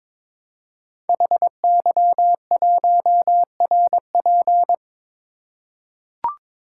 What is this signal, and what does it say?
Morse code, heard as a keyed tone, about 22 wpm, 700 Hz. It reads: HY1RP
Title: Code Groups - 5 Letter:Numbers - 22wpm